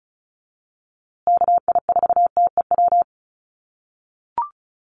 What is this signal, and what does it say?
Morse code, heard as a keyed tone, about 34 wpm, 700 Hz. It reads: KI4TEW
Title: USA Call Signs Repeated 3x 35wpm